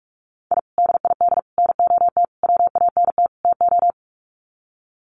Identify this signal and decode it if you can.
Morse code, heard as a keyed tone, about 44 wpm, 700 Hz. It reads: I DID NOT WANT TO
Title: Sets of 5 Words Encore 45wpm